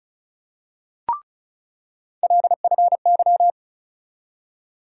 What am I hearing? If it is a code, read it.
Morse code, heard as a keyed tone, about 34 wpm, 700 Hz. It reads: LFY
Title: USA Call Sign Postfixes - 3 Letters 35wpm